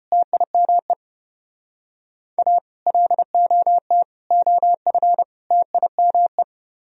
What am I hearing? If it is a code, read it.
Morse code, heard as a keyed tone, about 32 wpm, 700 Hz. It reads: TIME A LOT OF TIME